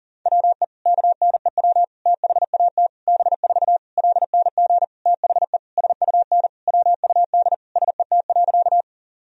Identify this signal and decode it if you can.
Morse code, heard as a keyed tone, about 40 wpm, 700 Hz. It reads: WE KNEW THAT B4 LNG THE SUN WUD SET.
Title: Sentences from Top 400 Words and Abbreviations 40wpm